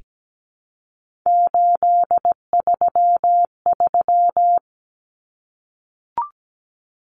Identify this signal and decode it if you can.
Morse code, heard as a keyed tone, about 17 wpm, 700 Hz. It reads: 833
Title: Numbers - 3-Digit 17wpm